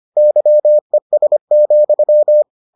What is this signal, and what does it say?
Morse code, heard as a keyed tone, about 25 wpm, 600 Hz. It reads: YES,